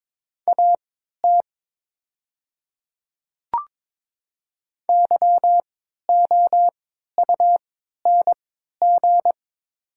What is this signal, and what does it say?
Morse code, heard as a keyed tone, about 22 wpm, spaced as for 12 wpm, 700 Hz. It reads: AT YOUNG
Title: 3x Character Spacing - Top 500 Words - 22wpm